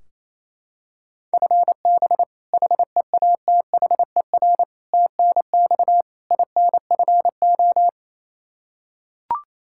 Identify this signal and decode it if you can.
Morse code, heard as a keyed tone, about 28 wpm, 700 Hz. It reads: FB HEATHER TNX INFO
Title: Practice QSO - Exchange 3 - More Info - 28wpm